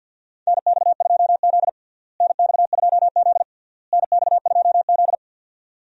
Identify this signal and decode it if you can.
Morse code, heard as a keyed tone, about 52 wpm, 700 Hz. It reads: NX2Z NX2Z NX2Z
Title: USA Call Signs - 4 Characters - Repeated 3x - 50wpm